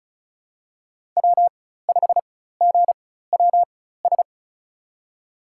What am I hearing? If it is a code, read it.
Morse code, heard as a keyed tone, about 36 wpm, spaced as for 16 wpm, 700 Hz. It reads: W5GWS
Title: USA Call Signs - 5 Characters - 4x Character Spacing - 35wpm